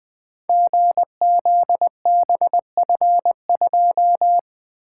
Morse code, heard as a keyed tone, about 20 wpm, 700 Hz. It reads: GZBF2
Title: Code Groups - 5 Letter:Numbers - 20wpm